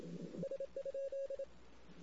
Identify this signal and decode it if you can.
Morse code, heard as a keyed tone, about 27 wpm, 550 Hz. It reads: S?
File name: CW / 2008-ss-cw-k5tr-op-n5rz -16:00